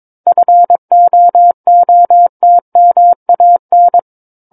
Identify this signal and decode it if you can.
Morse code, heard as a keyed tone, about 22 wpm, 700 Hz. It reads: FOOTMAN
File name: 06-Chapter-06 22 WPM